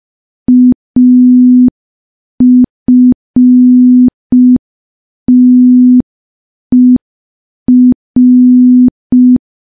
Morse code, heard as a keyed tone, about 5 wpm, 250 Hz. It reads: AFTER
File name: morse / morse-training-05WPM